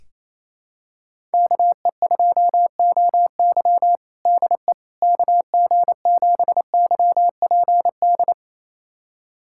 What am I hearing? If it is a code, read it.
Morse code, heard as a keyed tone, about 27 wpm, 700 Hz. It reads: KE2OY DE KG7YPD
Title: USA Callsign DE Callsign 28wpm